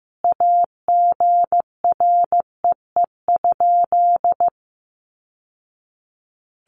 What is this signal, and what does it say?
Morse code, heard as a keyed tone, about 15 wpm, 700 Hz. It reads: AGREE?
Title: Sentences from Top 1100 Words 15wpm